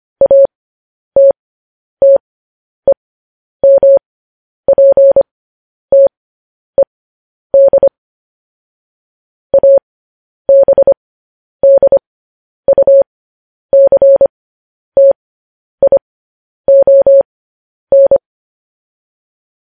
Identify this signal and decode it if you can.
Morse code, heard as a keyed tone, about 25 wpm, spaced as for 10 wpm, 550 Hz. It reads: ATTEMPTED ABDUCTION
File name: podcast / News headlines-1x-25x10